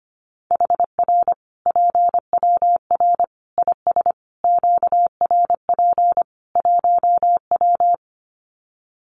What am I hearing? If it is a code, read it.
Morse code, heard as a keyed tone, about 25 wpm, 700 Hz. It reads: HR PWR IS QRP 1W
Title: Common QSO Phrases 25wpm